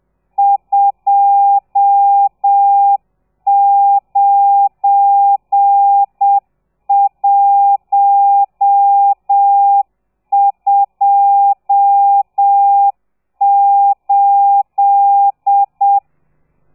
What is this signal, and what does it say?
Morse code, heard as a keyed tone, about 7 wpm, 800 Hz. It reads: 29128
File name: telegrafia / i2tfj / Lez12 30